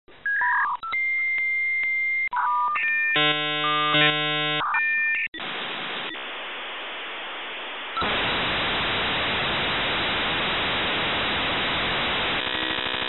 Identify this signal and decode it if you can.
Dial-up modem handshake.